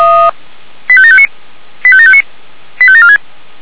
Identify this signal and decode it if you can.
Five-tone selective call (ZVEI), heard as quick runs of five tones, tones of about 70 ms. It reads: 85759 85759 85746